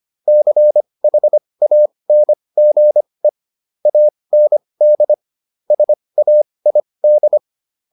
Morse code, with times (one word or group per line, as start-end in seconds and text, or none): 0.3-3.3 CHANGE
3.9-5.1 AND
5.7-7.4 SAID